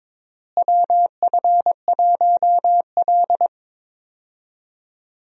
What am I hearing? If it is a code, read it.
Morse code, heard as a keyed tone, about 22 wpm, 700 Hz. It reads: WF1L
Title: USA Call Signs - 4 Characters - Repeated 3x - 22wpm